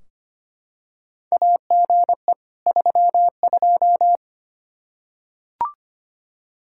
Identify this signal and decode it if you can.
Morse code, heard as a keyed tone, about 25 wpm, 700 Hz. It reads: AGE 32
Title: K1USN SST QSO Practice 25wpm